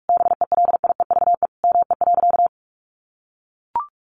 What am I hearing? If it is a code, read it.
Morse code, heard as a keyed tone, about 44 wpm, 700 Hz. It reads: BELIEVE ME.
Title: Sentences from Top 1000 Words 45wpm